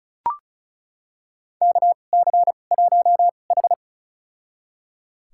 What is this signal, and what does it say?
Morse code, heard as a keyed tone, about 35 wpm, spaced as for 25 wpm, 700 Hz. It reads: KC1H
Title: USA Call Signs - 4 Characters - 2x Character Spacing - 35wpm